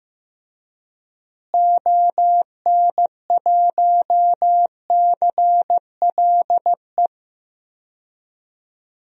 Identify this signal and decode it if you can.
Morse code, heard as a keyed tone, about 15 wpm, 700 Hz. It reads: ON1CLE
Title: DXCC-100 Call Signs - 6 Characters - Repeated 2x - 15wpm